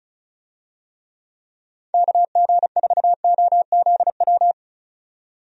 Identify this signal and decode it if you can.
Morse code, heard as a keyed tone, about 36 wpm, 700 Hz. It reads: KG4OZW
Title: USA Call Signs - 6 Characters - 35wpm